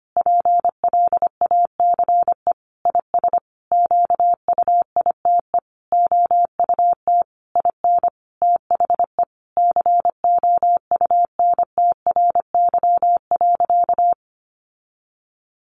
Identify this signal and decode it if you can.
Morse code, heard as a keyed tone, about 25 wpm, 700 Hz. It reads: PLACE IS QUITE OUT IN THE COUNTRY.
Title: Sentences from Top 700 Words 25wpm